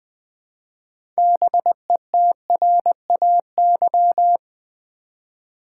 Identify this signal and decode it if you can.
Morse code, heard as a keyed tone, about 20 wpm, 700 Hz. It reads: BETRAY
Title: 6 Letter Words - First 3 Letters Given - 20wpm